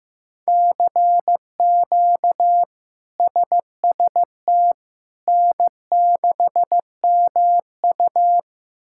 Morse code, heard as a keyed tone, about 16 wpm, 700 Hz. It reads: CQ SST N6MU
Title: K1USN SST QSO Practice - 15wpm